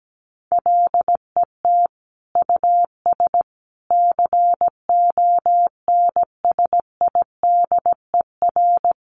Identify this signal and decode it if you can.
Morse code, heard as a keyed tone, about 17 wpm, 700 Hz. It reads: LET US CONSIDER
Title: Sentences from Top 1300 Words 17wpm